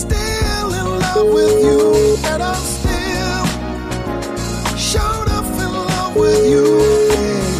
Ringback tone (Europe, 425 Hz), with noise.